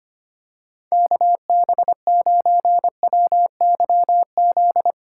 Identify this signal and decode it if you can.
Morse code, heard as a keyed tone, about 25 wpm, 700 Hz. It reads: KB9WYZ
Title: USA Call Signs - 6 Characters - 8x Character Spacing - 25wpm